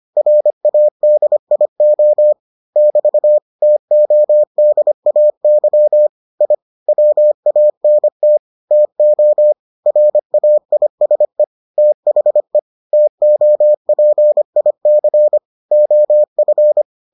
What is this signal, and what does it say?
Morse code, heard as a keyed tone, about 25 wpm, 600 Hz. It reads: RADIO <BT> TODAY I WANT TO RAISE THE TOPIC OF